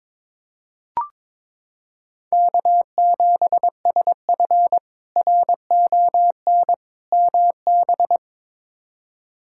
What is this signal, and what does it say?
Morse code, heard as a keyed tone, about 22 wpm, 700 Hz. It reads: K7SF RON MB
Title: K1USN SST QSO Practice - 22wpm